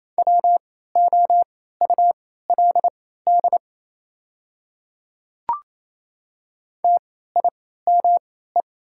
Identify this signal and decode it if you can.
Morse code, heard as a keyed tone, about 27 wpm, spaced as for 16 wpm, 690 Hz. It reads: WOULD TIME
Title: 3x Character Spacing - Top 100 Words - 28wpm